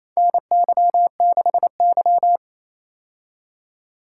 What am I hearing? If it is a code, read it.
Morse code, heard as a keyed tone, about 28 wpm, 700 Hz. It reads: NY6Y
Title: K1USN SST QSO Practice - 28wpm